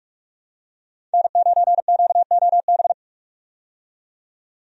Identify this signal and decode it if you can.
Morse code, heard as a keyed tone, about 46 wpm, 700 Hz. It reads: N9QOB
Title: USA Call Signs - 5 Characters - 45wpm